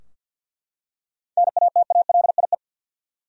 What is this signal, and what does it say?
Morse code, heard as a keyed tone, about 51 wpm, 700 Hz. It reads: NATALIE
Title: USA Common Names - 50wpm